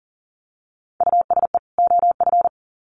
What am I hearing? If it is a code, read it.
Morse code, heard as a keyed tone, about 40 wpm, 700 Hz. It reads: USE OF
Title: Sets of 2 Words 40wpm